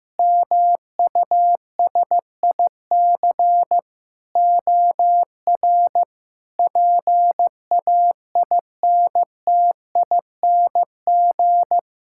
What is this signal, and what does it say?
Morse code, heard as a keyed tone, about 15 wpm, 700 Hz. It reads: MUSIC OR PAINTING
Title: Sentences from Top 1000 Words - Review Part 6 - 15wpm